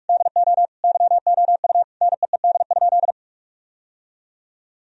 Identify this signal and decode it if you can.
Morse code, heard as a keyed tone, about 43 wpm, 700 Hz. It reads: DO YOU NEED?
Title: Sentences from Top 1000 Words - Review Part 4 - 45wpm